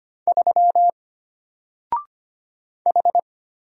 Morse code, heard as a keyed tone, about 25 wpm, 700 Hz. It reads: 3 H
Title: Single Character (Letter:Number:Punctuation:Prosign) Alt 25wpm